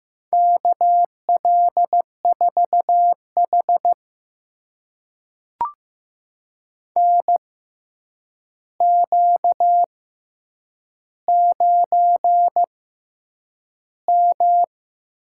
Morse code, heard as a keyed tone, about 15 wpm, spaced as for 8 wpm, 700 Hz. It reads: KL4H NQ9M
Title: USA Call Signs - 4 Characters - 6x Character Spacing - 15wpm